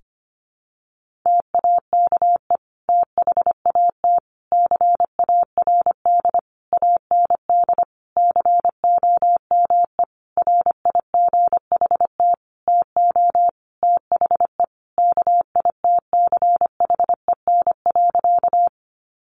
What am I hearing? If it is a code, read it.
Morse code, heard as a keyed tone, about 25 wpm, 700 Hz. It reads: TAKE THAT CARD AND COME RIGHT TO THE KITCHEN.
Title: Sentences from Top 1400 Words 25wpm